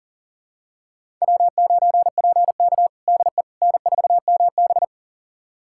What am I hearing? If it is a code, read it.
Morse code, heard as a keyed tone, about 40 wpm, 700 Hz. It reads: W9PK DE N4MB
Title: USA Callsign DE Callsign - 4 Characters - 40wpm